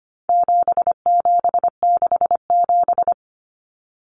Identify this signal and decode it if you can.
Morse code, heard as a keyed tone, about 25 wpm, 700 Hz. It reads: 7767
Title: Numbers - 4-Digit 25wpm